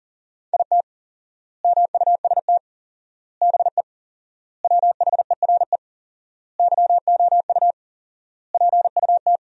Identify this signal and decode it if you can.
Morse code, heard as a keyed tone, about 40 wpm, 700 Hz. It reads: IT MUST BE WHERE YOU PUT
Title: 4x Word-Spacing - Sentences from Top 200 Words - 40wpm